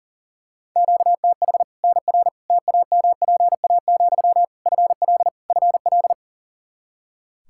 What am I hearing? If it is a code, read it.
Morse code, heard as a keyed tone, about 40 wpm, 700 Hz. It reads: QTH NR TAMPA, FL FL
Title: K1USN SST QSO Practice 40wpm